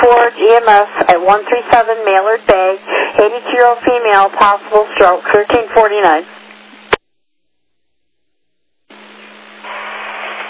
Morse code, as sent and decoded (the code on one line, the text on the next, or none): none